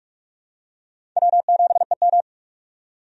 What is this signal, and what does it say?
Morse code, heard as a keyed tone, about 46 wpm, 700 Hz. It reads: W7EM
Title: USA Call Signs - 4 Characters - 45wpm